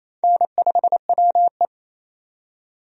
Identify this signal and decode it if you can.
Morse code, heard as a keyed tone, about 28 wpm, 700 Hz. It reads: N5WE
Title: USA Call Signs - 4 Characters - 28wpm